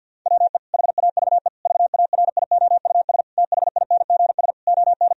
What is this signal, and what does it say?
Morse code, heard as a keyed tone, about 49 wpm, 700 Hz. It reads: WE HAVE VARIOUS THINGS ON
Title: Sentences from Top 1000 Words - Review Part 4 - 50wpm